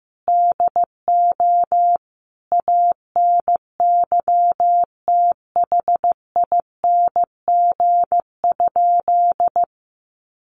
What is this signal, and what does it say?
Morse code, heard as a keyed tone, about 15 wpm, 700 Hz. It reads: DO ANYTHING?
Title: Sentences from Top 1000 Words 15wpm